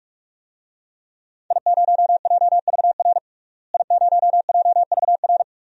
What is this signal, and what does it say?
Morse code, heard as a keyed tone, about 46 wpm, 700 Hz. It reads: I0JVR I0JVR
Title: DXCC-100 Call Signs - 5 Characters - Repeated 3x - 45wpm